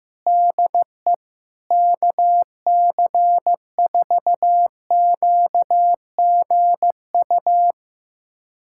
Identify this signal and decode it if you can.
Morse code, heard as a keyed tone, about 15 wpm, 700 Hz. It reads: DE KC4QGU
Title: USA Callsign DE Callsign - 6 Characters - 15wpm